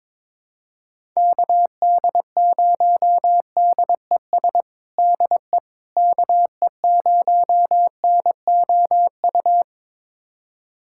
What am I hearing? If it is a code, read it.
Morse code, heard as a keyed tone, about 22 wpm, 700 Hz. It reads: KD0DES DE KE0NOU
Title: USA Callsign DE Callsign - 6 Characters - 22wpm